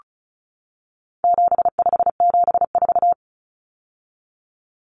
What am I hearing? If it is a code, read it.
Morse code, heard as a keyed tone, about 34 wpm, 700 Hz. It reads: 7574
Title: Common Cut Numbers - 4-Digit 35wpm